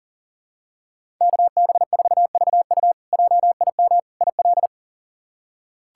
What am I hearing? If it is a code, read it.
Morse code, heard as a keyed tone, about 40 wpm, 700 Hz. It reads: KB4VU JIM IL
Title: K1USN SST QSO Practice - 40wpm